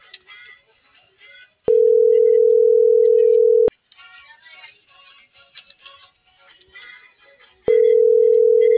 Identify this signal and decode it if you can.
Ringback tone (North America), 440 + 480 Hz.